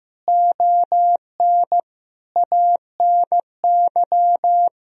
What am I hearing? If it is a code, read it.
Morse code, heard as a keyed tone, about 15 wpm, 700 Hz. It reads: ON ANY
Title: Sentences from Top 1000 Words - Review Part 2 - 15wpm